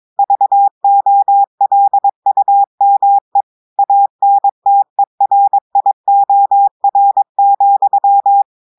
Morse code, heard as a keyed tone, about 22 wpm, 800 Hz. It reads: VOLUME ANTERIOR,
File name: audios / portugais / portugais texte en clair 1 vitesse 22 mots minutes